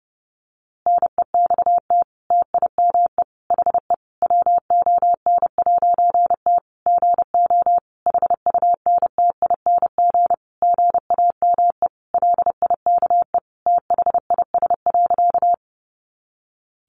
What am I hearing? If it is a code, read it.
Morse code, heard as a keyed tone, about 30 wpm, 700 Hz. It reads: NEXT TIME HE WON'T GO HUNTING GAME LIKE THIS.
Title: A Sound Of Thunder 30wpm